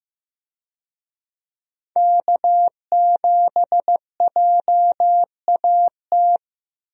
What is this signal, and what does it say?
Morse code, heard as a keyed tone, about 15 wpm, 700 Hz. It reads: K7JAT
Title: USA Call Signs - 5 Characters - Repeated 2x - 15wpm